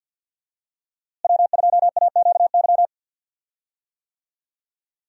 Morse code, heard as a keyed tone, about 52 wpm, 700 Hz. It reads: W2AQY
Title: USA Call Signs - 5 Characters - Repeated 3x - 50wpm